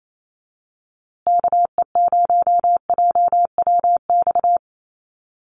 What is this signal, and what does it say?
Morse code, heard as a keyed tone, about 28 wpm, 700 Hz. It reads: KE0JWX
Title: USA Call Signs Repeated 2x 28wpm